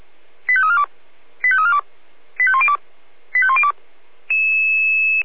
Five-tone selective call (ZVEI), heard as quick runs of five tones, tones of about 70 ms. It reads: 85342 85342 85102 85102